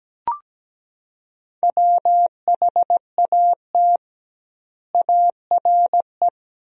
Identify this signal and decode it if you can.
Morse code, heard as a keyed tone, about 17 wpm, 700 Hz. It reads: WHAT ARE
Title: 2x Word-Spacing - Sentences from Top 100 Words - 17wpm